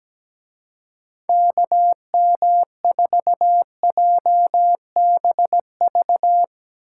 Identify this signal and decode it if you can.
Morse code, heard as a keyed tone, about 18 wpm, 700 Hz. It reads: KM4JBV
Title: Speed Racing - USA 2x3 Call Signs 17wpm